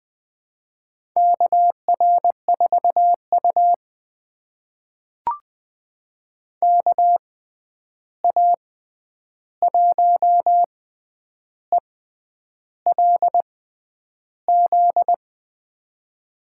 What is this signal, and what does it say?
Morse code, heard as a keyed tone, about 20 wpm, spaced as for 9 wpm, 700 Hz. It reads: KR4U KA1ELZ
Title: USA Call Signs - 6x Character Spacing - 20wpm